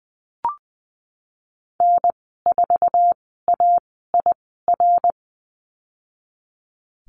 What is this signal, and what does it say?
Morse code, heard as a keyed tone, about 20 wpm, spaced as for 15 wpm, 700 Hz. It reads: N4AIR
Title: 062 Lesson 11g - Letter h - Call Signs 20wpm